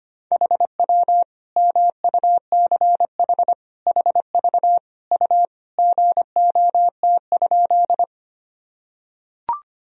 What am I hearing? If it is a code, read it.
Morse code, heard as a keyed tone, about 25 wpm, 700 Hz. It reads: HW MUCH HV U GOT?
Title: Sentences from Top 300 Words and Abbreviations 25wpm